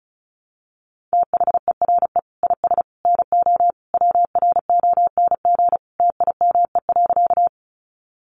Morse code, heard as a keyed tone, about 34 wpm, 700 Hz. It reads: THERE IS NO WRONG TIME.
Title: Sentences from Top 900 Words 35wpm